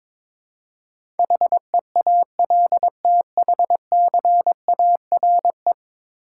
Morse code, heard as a keyed tone, about 22 wpm, 700 Hz. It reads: HEALTHCARE
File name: practice / 22wpm / 10 Letter Words - First 3 Letters Given - 22wpm